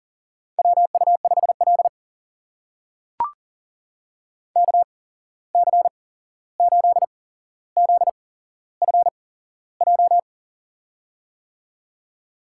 Morse code, heard as a keyed tone, about 40 wpm, spaced as for 15 wpm, 700 Hz. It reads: WU5L KC8ZFJ
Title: USA Call Signs - 8x Character Spacing - 40wpm